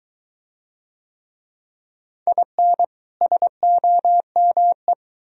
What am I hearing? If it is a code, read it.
Morse code, heard as a keyed tone, about 23 wpm, 700 Hz. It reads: IN SOME